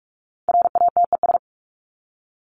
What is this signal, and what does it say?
Morse code, heard as a keyed tone, about 44 wpm, 700 Hz. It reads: RATES
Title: ES:S Suffix - Top 100 Words with Base Words 45wpm